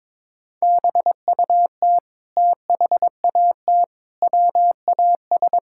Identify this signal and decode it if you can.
Morse code, heard as a keyed tone, about 22 wpm, 700 Hz. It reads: BUT THAT WAS
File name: practice / 22wpm / Sets of 8 Words - 22wpm